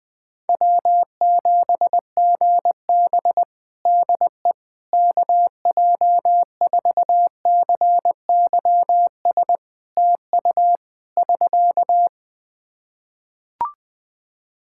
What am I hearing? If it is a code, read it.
Morse code, heard as a keyed tone, about 21 wpm, 700 Hz. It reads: W7GB DE KJ4CYS TU <SK>
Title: Practice QSO - Exchange 4 - Ending - 20wpm